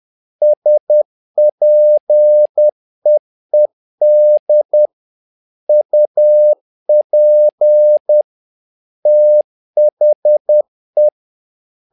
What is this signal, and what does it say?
Morse code, heard as a keyed tone, about 10 wpm, 600 Hz. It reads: SPEED UP THE
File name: learning center / tower trouble / tower trouble 10wpm